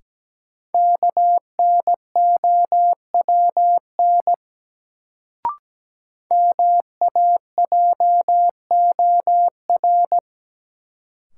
Morse code, heard as a keyed tone, about 17 wpm, 700 Hz. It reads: KNOWN MAJOR
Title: Sentences from Top 600 Words - New Word Drill - 17wpm